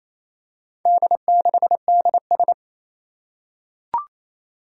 Morse code, heard as a keyed tone, about 29 wpm, 700 Hz. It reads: D6DS
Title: Four Challenging Letter-Number Combinations 28wpm